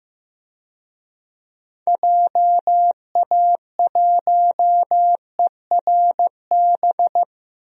Morse code, heard as a keyed tone, about 15 wpm, 700 Hz. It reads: JA1ERB
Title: DXCC-100 Call Signs - 6 Characters - Repeated 3x - 15wpm